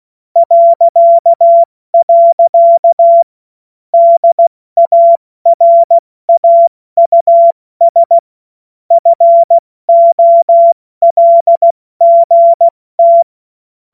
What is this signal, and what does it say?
Morse code, heard as a keyed tone, about 16 wpm, 680 Hz. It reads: .. DARAUS FOLGT
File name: Morsekurs / Witze / Witz 07. Dezember 2021, 70 F80